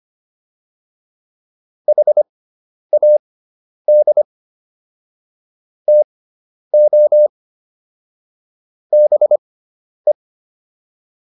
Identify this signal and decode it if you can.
Morse code, heard as a keyed tone, about 25 wpm, spaced as for 10 wpm, 600 Hz. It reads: HAD TO BE